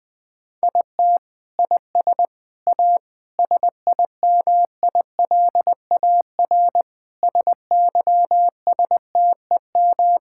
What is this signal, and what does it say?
Morse code, heard as a keyed tone, about 21 wpm, 700 Hz. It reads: IT IS A SIMILAR SYSTEM
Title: Sentences from Top 1000 Words - Review Part 5 - 20wpm